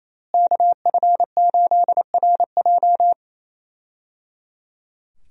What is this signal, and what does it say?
Morse code, heard as a keyed tone, about 27 wpm, 700 Hz. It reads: KF8RJ
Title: USA Call Signs - 5 Characters - Repeated 3x - 28wpm